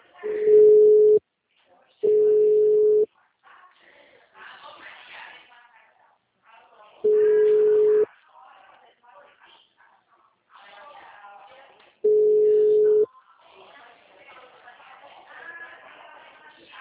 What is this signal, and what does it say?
Ringback tone (Europe, 425 Hz), with speech.